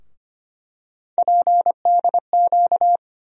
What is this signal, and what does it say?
Morse code, heard as a keyed tone, about 25 wpm, 700 Hz. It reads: PDQ
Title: USA Call Sign Postfixes - 3 Letters 25wpm